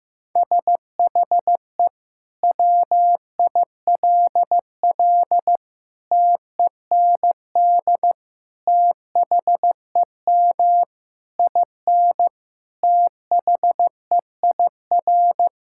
Morse code, heard as a keyed tone, about 15 wpm, 700 Hz. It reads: SHE WILL TEND THEM IN THEIR
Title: Sentences from Top 1000 Words - Review Part 5 - 15wpm